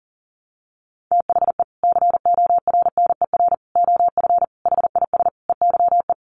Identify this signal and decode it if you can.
Morse code, heard as a keyed tone, about 40 wpm, 700 Hz. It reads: THE CORNER OF HIS EYE
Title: Sets of 5 Words Encore2 40wpm